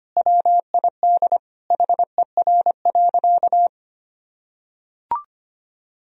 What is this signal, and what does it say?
Morse code, heard as a keyed tone, about 25 wpm, 700 Hz. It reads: WID HER.
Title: Sentences from Top 100 Words and Abbreviations 25wpm